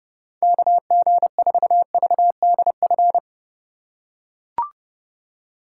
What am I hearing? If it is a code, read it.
Morse code, heard as a keyed tone, about 30 wpm, 700 Hz. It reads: KG4VDF